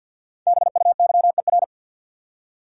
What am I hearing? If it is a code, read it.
Morse code, heard as a keyed tone, about 52 wpm, 700 Hz. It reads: BUYER
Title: Five Letter Words - 50wpm